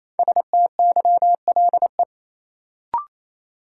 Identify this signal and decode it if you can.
Morse code, heard as a keyed tone, about 28 wpm, 700 Hz. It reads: STYLE
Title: Sentences from Top 800 Words - New Word Drill - 28wpm